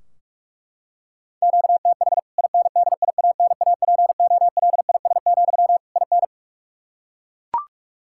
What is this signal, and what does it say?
Morse code, heard as a keyed tone, about 41 wpm, 700 Hz. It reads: QTH INDIANAPOLIS, IN
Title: K1USN SST QSO Practice 45wpm